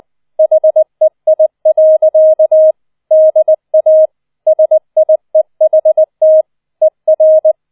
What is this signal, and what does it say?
Morse code, heard as a keyed tone, about 20 wpm, 630 Hz. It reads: HEI. DA SIEHT ER